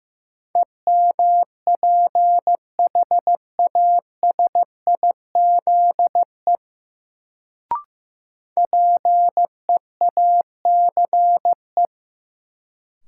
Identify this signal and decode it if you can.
Morse code, heard as a keyed tone, about 15 wpm, 700 Hz. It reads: EMPHASIZE PEACE
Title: Sentences from Top 1500 Words - New Word Drill - 15wpm